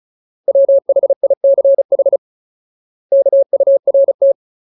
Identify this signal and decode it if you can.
Morse code, heard as a keyed tone, about 34 wpm, 550 Hz. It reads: WHICH KURT
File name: Ultimate Word List / 35WPM Ultimate